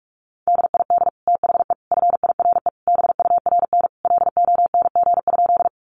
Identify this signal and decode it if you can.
Morse code, heard as a keyed tone, about 44 wpm, 700 Hz. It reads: DID THE FIRE BURN LONG?
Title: Sentences from Top 1300 Words 45wpm